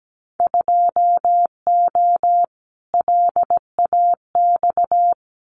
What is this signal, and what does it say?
Morse code, heard as a keyed tone, about 17 wpm, 700 Hz. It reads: 2O LAX
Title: Winter Field Day Exchanges 17wpm